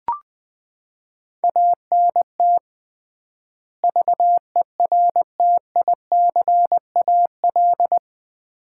Morse code, heard as a keyed tone, about 21 wpm, 700 Hz. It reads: ANT VERTICAL
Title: Common QSO Phrases - 3x Word-Spacing - 20wpm